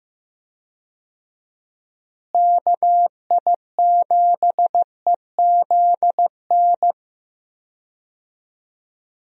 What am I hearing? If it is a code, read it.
Morse code, heard as a keyed tone, about 15 wpm, 700 Hz. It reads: KI7EZN
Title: USA Call Signs - 6 Characters - Repeated 2x - 15wpm